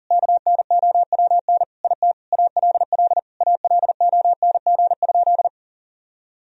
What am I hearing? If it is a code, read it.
Morse code, heard as a keyed tone, about 40 wpm, 700 Hz. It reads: KNOWN IT ALL ALONG?
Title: Sentences from Top 1000 Words - Review Part 3 - 40wpm